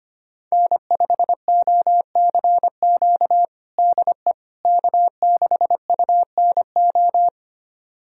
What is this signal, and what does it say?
Morse code, heard as a keyed tone, about 25 wpm, 700 Hz. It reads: N5OCQ DE K6UNO
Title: USA Callsign DE Callsign - 5 Characters - 25wpm